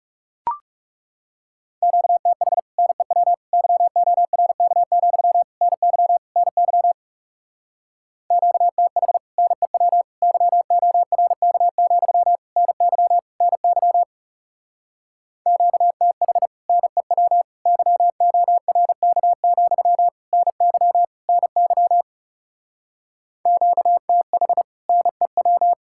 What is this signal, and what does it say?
Morse code, heard as a keyed tone, about 38 wpm, 700 Hz. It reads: QTH NEW YORK, NY NY QTH NEW YORK, NY NY QTH NEW YORK, NY NY QTH NEW